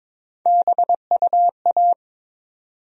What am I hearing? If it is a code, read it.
Morse code, heard as a keyed tone, about 22 wpm, 700 Hz. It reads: BUA
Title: USA Call Sign Postfixes - 3 Letters 22wpm